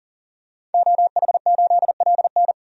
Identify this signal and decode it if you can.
Morse code, heard as a keyed tone, about 40 wpm, 700 Hz. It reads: OH8LN
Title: DXCC-100 Call Signs - 5 Characters - 40wpm